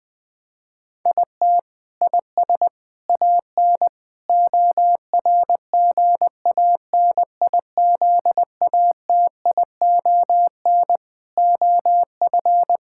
Morse code, heard as a keyed tone, about 20 wpm, 700 Hz. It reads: IT IS AN ORGANIZATION OF
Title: Sentences from Top 1000 Words - Review Part 6 - 20wpm